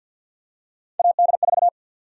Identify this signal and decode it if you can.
Morse code, heard as a keyed tone, about 48 wpm, 700 Hz. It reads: AD4
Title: USA Call Sign Prefixes - Repeated 2x - 50wpm